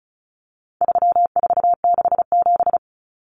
Morse code, heard as a keyed tone, about 34 wpm, 700 Hz. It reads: 3467
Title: Numbers - 4-Digit 35wpm